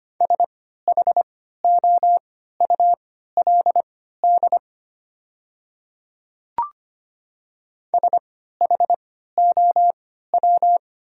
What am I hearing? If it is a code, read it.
Morse code, heard as a keyed tone, about 25 wpm, spaced as for 14 wpm, 700 Hz. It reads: SHOULD SHOW
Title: 3x Character Spacing - Top 300 Words - 25wpm